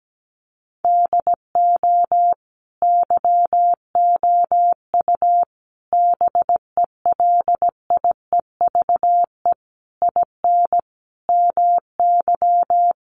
Morse code, heard as a keyed tone, about 17 wpm, 700 Hz. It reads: DO YOU BELIEVE IN MY
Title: Sentences from Top 1200 Words 17wpm